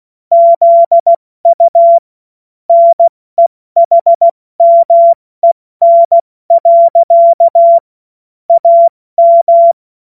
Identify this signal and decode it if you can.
Morse code, heard as a keyed tone, about 16 wpm, 680 Hz. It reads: ZU NEHMEN. AM
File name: Morsekurs / Witze / Witz 15. November 2022 70 F80